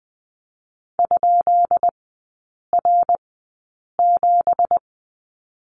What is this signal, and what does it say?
Morse code, heard as a keyed tone, about 20 wpm, spaced as for 8 wpm, 700 Hz. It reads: ?R7
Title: 205 Lesson 33e - Letter j - Three Characters 20wpm